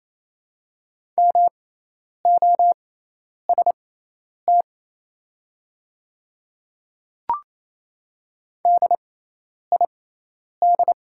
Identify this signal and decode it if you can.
Morse code, heard as a keyed tone, about 27 wpm, spaced as for 10 wpm, 700 Hz. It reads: MOST DID
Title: 6x Character Spacing - Top 100 Words - 28wpm